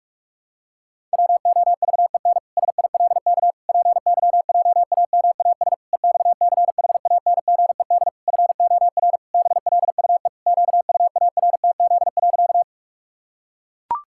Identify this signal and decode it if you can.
Morse code, heard as a keyed tone, about 41 wpm, 700 Hz. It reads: WOVEN SILK PYJAMAS EXCHANGED FOR BLUE QUARTZ.
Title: Pangram Challenge v2 - 45wpm